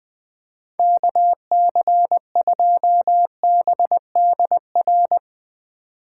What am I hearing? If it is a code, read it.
Morse code, heard as a keyed tone, about 20 wpm, 700 Hz. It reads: KC2BDR
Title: Speed Racing - USA 2x3 Call Signs 17wpm